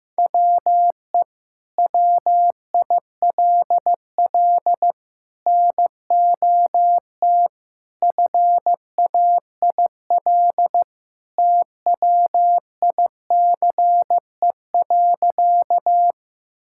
Morse code, heard as a keyed tone, about 15 wpm, 700 Hz. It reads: WE WILL NOT FAIL TWICE.
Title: Sentences from Top 1600 Words 15wpm